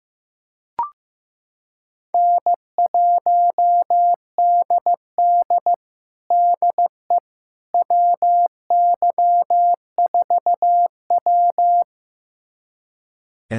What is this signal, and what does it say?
Morse code, heard as a keyed tone, about 15 wpm, 700 Hz. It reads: N1DD DE WY4W
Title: USA Callsign DE Callsign - 4 Characters - 15wpm